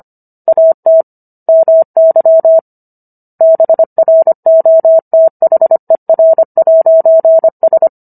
Morse code, heard as a keyed tone, about 25 wpm, 650 Hz. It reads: AT MY BROTHER'S